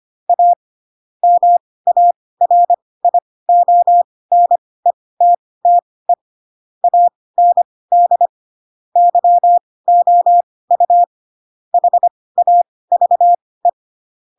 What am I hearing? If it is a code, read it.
Morse code, heard as a keyed tone, about 25 wpm, spaced as for 18 wpm, 700 Hz. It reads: A MARIONETTE AND YOU HAVE
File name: Pinocchio / Pinocchio ch7 12 25 ik1hge